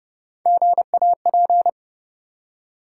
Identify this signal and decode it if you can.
Morse code, heard as a keyed tone, about 30 wpm, 700 Hz. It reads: GAP
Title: Three Letter Words - 30wpm